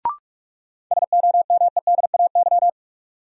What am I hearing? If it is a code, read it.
Morse code, heard as a keyed tone, about 46 wpm, 700 Hz. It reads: SOMEDAY
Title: Sentences from Top 1600 Words - New Word Drill Alt - 45wpm